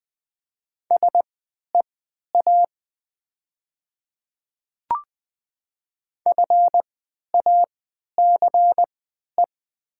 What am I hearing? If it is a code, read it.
Morse code, heard as a keyed tone, about 20 wpm, spaced as for 11 wpm, 700 Hz. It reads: SEA FACE
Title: 3x Character Spacing - Top 300 Words - 20wpm